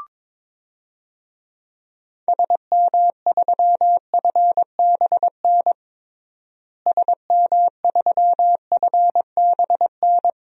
Morse code, heard as a keyed tone, about 22 wpm, 700 Hz. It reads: SM3FBN SM3FBN
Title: DXCC-100 Call Signs - 6 Characters - Repeated 2x - 22wpm